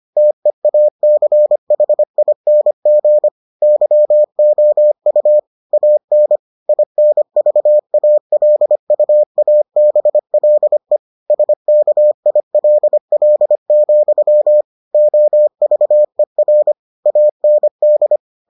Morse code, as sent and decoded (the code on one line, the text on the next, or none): - . .- -.-. .... .. -. --. / -.-- --- ..- / .- -. / .. -. ...- .- .-.. ..- .- -... .-.. / ... -.- .. .-.. .-.. --..-- / --- ...- . .-. / .- -. -..
TEACHING YOU AN INVALUABLE SKILL, OVER AND